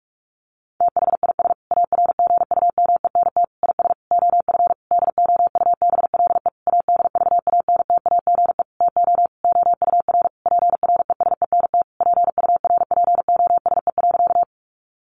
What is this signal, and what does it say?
Morse code, heard as a keyed tone, about 44 wpm, 700 Hz. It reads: THIS ARGUMENT IS OF DOUBLE ADVANTAGE TO OUR PRESENT PURPOSE.
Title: Sentences from Top 1400 Words 45wpm